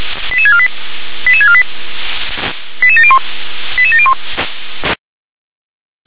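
Five-tone selective call (ZVEI), heard as quick runs of five tones, tones of about 70 ms. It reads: 88648 88648 88701 88701
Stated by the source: ZVEI five-tone selective call received by SDR.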